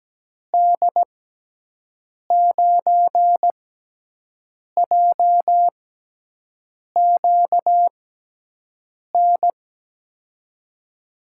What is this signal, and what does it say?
Morse code, heard as a keyed tone, about 17 wpm, 700 Hz. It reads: D 9 J Q N
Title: USA Call Signs - 6 Characters - 6x Character Spacing - 17wpm